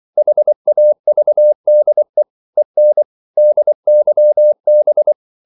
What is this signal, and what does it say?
Morse code, heard as a keyed tone, about 24 wpm, 600 Hz. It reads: HAVDE EN DYB
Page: Ferie 23 wpm